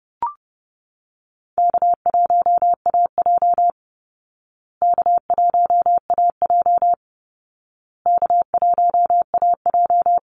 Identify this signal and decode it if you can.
Morse code, heard as a keyed tone, about 30 wpm, 700 Hz. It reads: K1AJ K1AJ K1AJ
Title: Call Signs Repeated 2x - 30wpm